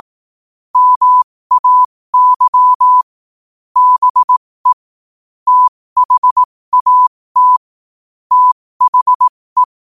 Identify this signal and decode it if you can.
Morse code, heard as a keyed tone, about 18 wpm, spaced as for 15 wpm, 1000 Hz. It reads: MAY BE THAT THE
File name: qotdfiles / 15wpm